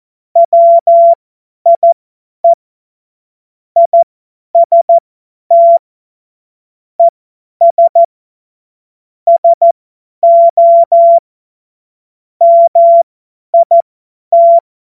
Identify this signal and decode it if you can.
Morse code, heard as a keyed tone, about 14 wpm, spaced as for 10 wpm, 680 Hz. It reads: WIE IST ES SO MIT
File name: Morsekurs / Witze / Witz 31. August 2021, 50 F70